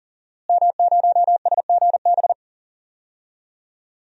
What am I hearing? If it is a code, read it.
Morse code, heard as a keyed tone, about 39 wpm, 700 Hz. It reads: M0SGB
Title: DXCC-100 Call Signs - 5 Characters - Repeated 3x - 40wpm